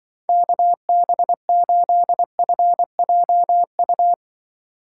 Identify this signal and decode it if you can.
Morse code, heard as a keyed tone, about 25 wpm, 700 Hz. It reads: KB8FJU
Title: Speed Racing - USA 2x3 Call Signs 20wpm